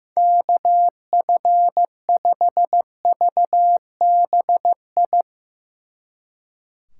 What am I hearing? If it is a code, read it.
Morse code, heard as a keyed tone, about 16 wpm, 700 Hz. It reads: KF5VBI
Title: Speed Racing - USA 2x3 Call Signs 15wpm